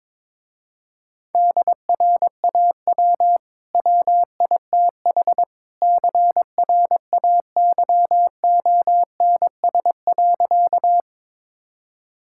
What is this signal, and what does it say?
Morse code, heard as a keyed tone, about 21 wpm, 700 Hz. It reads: DRAW WITH CRAYONS.
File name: practice / 22wpm / Easy Sentences with Hints - 22wpm